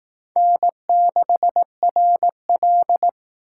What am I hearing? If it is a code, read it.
Morse code, heard as a keyed tone, about 19 wpm, 700 Hz. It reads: N6RL